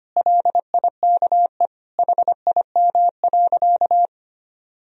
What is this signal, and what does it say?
Morse code, heard as a keyed tone, about 25 wpm, 700 Hz. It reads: LIKE HIM.
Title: Sentences from Top 1000 Words - Review Part 1 - 25wpm